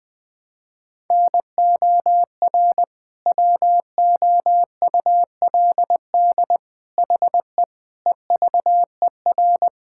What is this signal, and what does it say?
Morse code, heard as a keyed tone, about 20 wpm, 700 Hz. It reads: NOR WOULD HE EVER
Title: Sentences from Top 1600 Words 20wpm